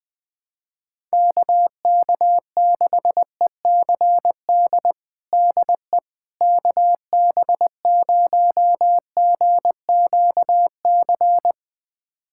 Morse code, heard as a keyed tone, about 21 wpm, 700 Hz. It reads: KK6ECD DE KB0GQC
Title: USA Callsign DE Callsign - 6 Characters - 20wpm